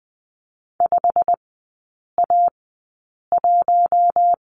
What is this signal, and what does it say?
Morse code, heard as a keyed tone, about 20 wpm, spaced as for 8 wpm, 700 Hz. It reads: 5A1
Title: 127 Lesson 21e - Number 6 - Three Characters 20wpm